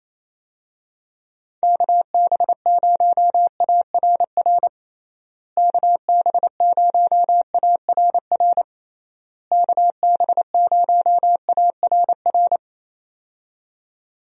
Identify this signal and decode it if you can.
Morse code, heard as a keyed tone, about 28 wpm, 700 Hz. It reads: KB0ARR KB0ARR KB0ARR
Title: USA Call Signs - 6 Characters - Repeated 3x - 28wpm